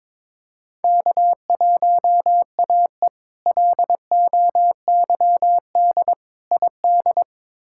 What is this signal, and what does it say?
Morse code, heard as a keyed tone, about 22 wpm, 700 Hz. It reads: K1AE LOYD ID
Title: K1USN SST QSO Practice - 22wpm